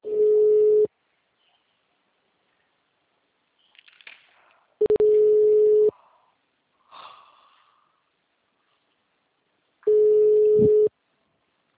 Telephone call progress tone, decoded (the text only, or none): ringback tone